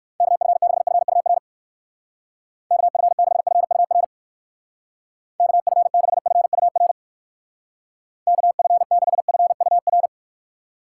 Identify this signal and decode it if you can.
Morse code, heard as a keyed tone, about 60 wpm, 700 Hz. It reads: KF6FUR KF6FUR KF6FUR KF6FUR